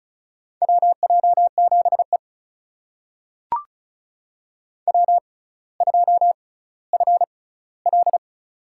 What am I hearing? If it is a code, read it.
Morse code, heard as a keyed tone, about 36 wpm, spaced as for 18 wpm, 700 Hz. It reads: WJ7E W2FL
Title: USA Call Signs - 4 Characters - 6x Character Spacing - 35wpm